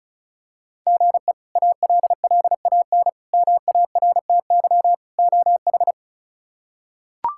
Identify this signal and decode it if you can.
Morse code, heard as a keyed tone, about 34 wpm, 700 Hz. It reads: GE ALLAN MARTY OH
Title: K1USN SST QSO Practice - 35wpm